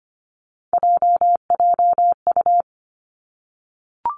Morse code, heard as a keyed tone, about 25 wpm, 700 Hz. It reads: JJU
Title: Three Challenging Letter-Number Combinations 25wpm